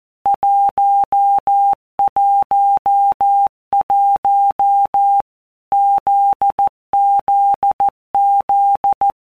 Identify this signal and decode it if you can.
Morse code, heard as a keyed tone, about 14 wpm, 800 Hz. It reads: 111 ZZZ
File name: Basic Letter Groups By WPM's / Groups@13 / JZ17@13